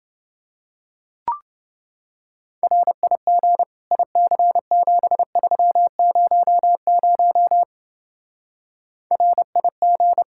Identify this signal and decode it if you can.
Morse code, heard as a keyed tone, about 28 wpm, 700 Hz. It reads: RIG IC7300 RIG